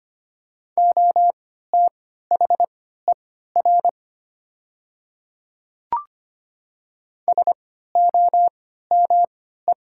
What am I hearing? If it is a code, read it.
Morse code, heard as a keyed tone, about 25 wpm, spaced as for 14 wpm, 700 Hz. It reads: OTHER SOME
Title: 3x Character Spacing - Top 100 Words - 25wpm